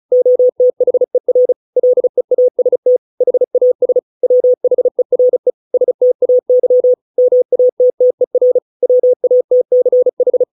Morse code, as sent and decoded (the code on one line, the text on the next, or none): --- - .... .-. / .-.. .- ... - / .... .- ... / .-- .... .-. / ... - .- -.-- / -- .- - - . .-. / .-- .- - -.-. ....
OTHER LEAST HAS WHERE STAY MATTER WATCH